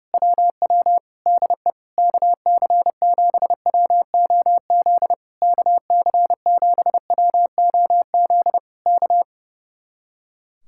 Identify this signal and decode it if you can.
Morse code, heard as a keyed tone, about 28 wpm, 700 Hz. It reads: WW DE KC7WOZ KC7WOZ K
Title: Speed-Racing Common QSO Phrases - 30wpm